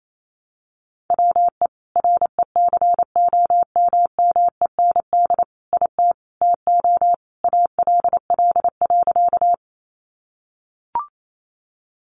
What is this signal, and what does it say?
Morse code, heard as a keyed tone, about 28 wpm, 700 Hz. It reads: WE RECOMMEND IT TO ALL.
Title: Sentences from Top 1200 Words 28wpm